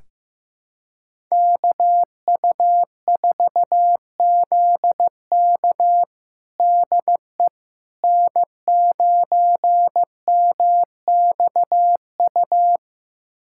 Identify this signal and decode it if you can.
Morse code, heard as a keyed tone, about 16 wpm, 700 Hz. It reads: KU4ZK DE N9MXU
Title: USA Callsign DE Callsign - 5 Characters - 15wpm